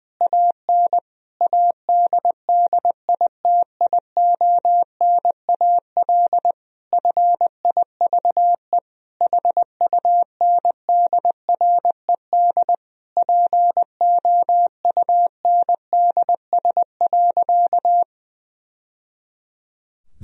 Morse code, heard as a keyed tone, about 20 wpm, 700 Hz. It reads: AN ADDITIONAL FIVE HUNDRED POUNDS.
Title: Sentences from Top 1000 Words - Review Part 4 - 20wpm